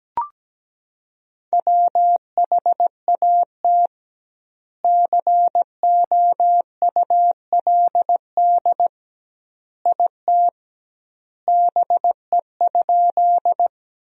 Morse code, heard as a keyed tone, about 17 wpm, 700 Hz. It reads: WHAT COULD IT BE?
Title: 2x Word-Spacing - Sentences from Top 100 Words - 17wpm